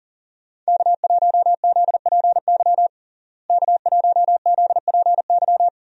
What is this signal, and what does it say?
Morse code, heard as a keyed tone, about 40 wpm, 700 Hz. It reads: K1ZPY K1ZPY
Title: USA Call Signs - 5 Characters - Repeated 3x - 40wpm